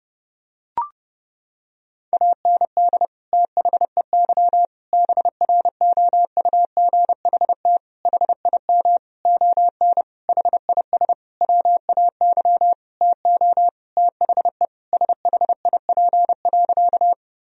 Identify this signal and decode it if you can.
Morse code, heard as a keyed tone, about 30 wpm, 700 Hz. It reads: AND THEY BROUGHT HIM ON HIS WAY TO THE SHIP.
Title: Sentences from Top 1000 Words - Review Part 2 - 30wpm